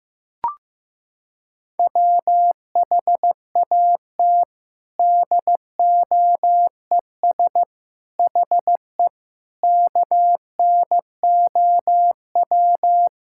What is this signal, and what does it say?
Morse code, heard as a keyed tone, about 15 wpm, 700 Hz. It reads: WHAT DOES HE KNOW
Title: Sentences from Top 1600 Words 15wpm